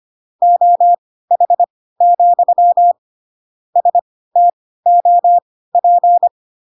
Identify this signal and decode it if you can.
Morse code, heard as a keyed tone, about 25 wpm, spaced as for 16 wpm, 700 Hz. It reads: OH, STOP
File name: Pinocchio / Pinocchio ch1 10 25 ik1hge